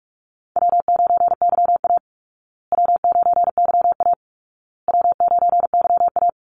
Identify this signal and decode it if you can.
Morse code, heard as a keyed tone, about 44 wpm, 700 Hz. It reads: W9YA W9YA W9YA
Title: Call Signs Repeated 2x - 45wpm